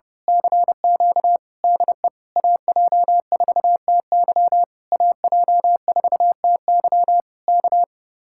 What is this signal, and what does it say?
Morse code, heard as a keyed tone, about 30 wpm, 700 Hz. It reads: CQ DE AJ4TY AJ4TY K
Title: Speed-Racing Common QSO Phrases Encore - 30wpm